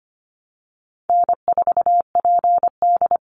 Morse code, heard as a keyed tone, about 25 wpm, 700 Hz. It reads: N4PD
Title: Easy Call Signs - 25wpm